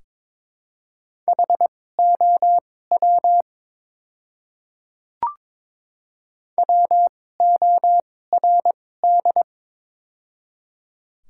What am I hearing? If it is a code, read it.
Morse code, heard as a keyed tone, about 21 wpm, spaced as for 16 wpm, 700 Hz. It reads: HOW WORD
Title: 2x Character Spacing - Top 100 Words - 22wpm